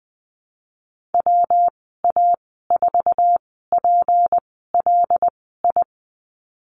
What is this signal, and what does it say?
Morse code, heard as a keyed tone, about 20 wpm, spaced as for 15 wpm, 700 Hz. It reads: WA4PLI
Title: 193 Lesson 31g - Letter v - Call Signs 20wpm